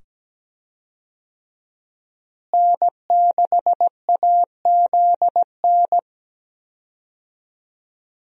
Morse code, heard as a keyed tone, about 17 wpm, 700 Hz. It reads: N6AZN
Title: USA Call Signs - 5 Characters - Repeated 2x - 17wpm